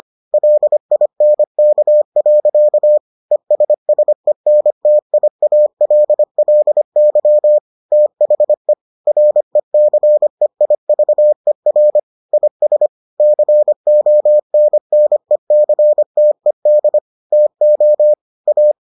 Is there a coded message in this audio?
Yes: Morse code from 0.3 to 18.7 s, heard as a keyed tone, about 25 wpm, 600 Hz. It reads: LINK. ESSENTIALLY THE RECEIVER IS CONNECTED TO A